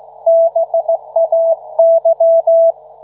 Morse code, heard as a keyed tone, about 16 wpm, 660 Hz. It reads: BAY